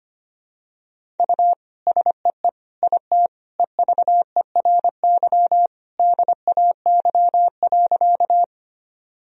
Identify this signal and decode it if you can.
Morse code, heard as a keyed tone, about 25 wpm, 700 Hz. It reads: U SEE IT EVERY DAY.
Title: Sentences from Top 200 Words and Abbreviations 25wpm